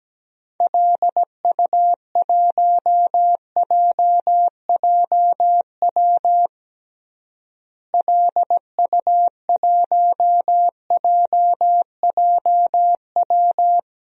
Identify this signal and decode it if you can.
Morse code, heard as a keyed tone, about 17 wpm, 700 Hz. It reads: LU1JJW LU1JJW
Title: DXCC-100 Call Signs - 6 Characters - Repeated 3x - 17wpm